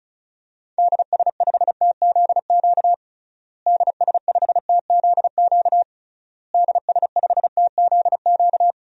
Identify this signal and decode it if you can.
Morse code, heard as a keyed tone, about 34 wpm, 700 Hz. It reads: DS5TZQ DS5TZQ DS5TZQ
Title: DXCC-100 Call Signs - 6 Characters - Repeated 3x - 35wpm